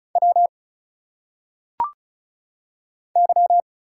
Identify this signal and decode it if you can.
Morse code, heard as a keyed tone, about 36 wpm, 700 Hz. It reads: W Y
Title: Single Letters - Phonetic Pronunciation 35wpm